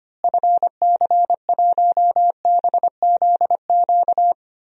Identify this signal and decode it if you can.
Morse code, heard as a keyed tone, about 25 wpm, 700 Hz. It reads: FC1BZQ
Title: DXCC-100 Call Signs - 6 Characters - Repeated 2x - 25wpm